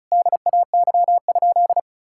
Morse code, heard as a keyed tone, about 36 wpm, 700 Hz. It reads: DAY?
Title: Sentences from Top 1000 Words - Review Part 5 - 35wpm